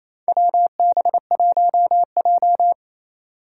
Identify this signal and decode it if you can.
Morse code, heard as a keyed tone, about 28 wpm, 700 Hz. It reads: WB1J
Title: USA Call Signs - 4 Characters - 4x Character Spacing - 28wpm